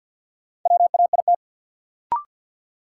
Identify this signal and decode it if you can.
Morse code, heard as a keyed tone, about 52 wpm, 700 Hz. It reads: WAIT
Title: Sets of 5 Words Encore - New Word Drill - 50wpm